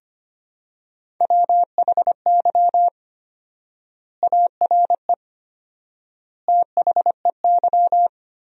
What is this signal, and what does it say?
Morse code, heard as a keyed tone, about 25 wpm, 700 Hz. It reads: WHY ARE THEY
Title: 4x Word-Spacing - Sentences from Top 200 Words - 25wpm